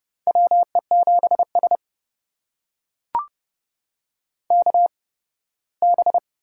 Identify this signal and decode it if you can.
Morse code, heard as a keyed tone, about 30 wpm, spaced as for 18 wpm, 700 Hz. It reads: WE7S KB
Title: USA Call Signs - 4 Characters - 8x Character Spacing - 30wpm